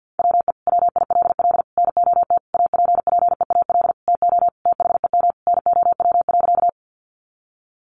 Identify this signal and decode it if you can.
Morse code, heard as a keyed tone, about 50 wpm, 700 Hz. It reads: WE WILL NOT APPEAL TO THEM NOW.
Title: Sentences from Top 1300 Words 50wpm